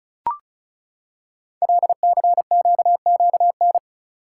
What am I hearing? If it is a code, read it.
Morse code, heard as a keyed tone, about 36 wpm, 690 Hz. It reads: LCQQN
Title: Code Groups - 5 Letters - 35wpm